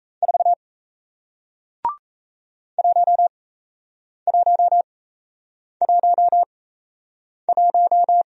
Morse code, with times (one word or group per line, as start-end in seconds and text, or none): 0.2-0.5 4
2.8-8.2 1111